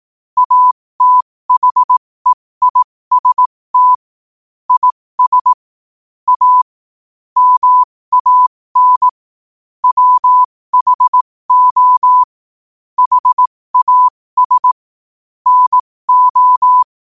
Morse code, heard as a keyed tone, about 18 wpm, spaced as for 15 wpm, 1000 Hz. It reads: ATHEIST IS A MAN WHO HAS NO
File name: qotdfiles / 15wpm